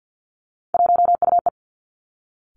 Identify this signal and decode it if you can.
Morse code, heard as a keyed tone, about 50 wpm, 700 Hz. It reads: 1UE
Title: Three Challenging Letter-Number Combinations 50wpm